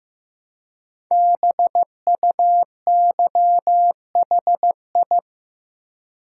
Morse code, heard as a keyed tone, about 15 wpm, 700 Hz. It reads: BUYHI
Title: Code Groups - 5 Letters - 15wpm